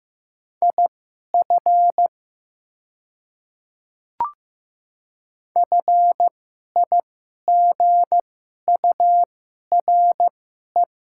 Morse code, heard as a keyed tone, about 15 wpm, spaced as for 11 wpm, 700 Hz. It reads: IF FIGURE
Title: 2x Character Spacing - Top 500 Words - 15wpm